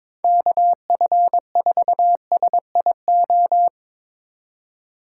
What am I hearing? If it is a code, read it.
Morse code, heard as a keyed tone, about 22 wpm, 700 Hz. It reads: KF4SIO
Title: USA Call Signs - 6 Characters - 6x Character Spacing - 22wpm